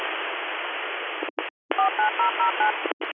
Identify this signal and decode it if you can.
DTMF, heard as telephone touch tones, tones of about 120 ms. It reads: 49009
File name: 101- .0-call 2366